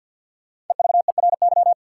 Morse code, heard as a keyed tone, about 50 wpm, 700 Hz. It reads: EVERY
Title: Five Letter Words - 50wpm